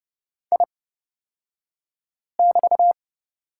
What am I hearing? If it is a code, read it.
Morse code, heard as a keyed tone, about 29 wpm, spaced as for 5 wpm, 690 Hz. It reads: I<BT>
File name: Lesson 36d - BT - Two Characters 30wpm